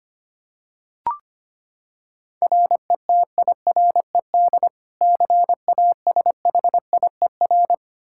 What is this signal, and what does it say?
Morse code, heard as a keyed tone, about 25 wpm, 700 Hz. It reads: RETIRED CASHIER
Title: Practice QSO - Exchange 3 - More Info - 25wpm